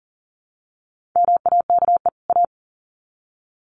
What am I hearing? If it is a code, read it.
Morse code, heard as a keyed tone, about 40 wpm, 700 Hz. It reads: MAKE A
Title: Sets of 2 Words 40wpm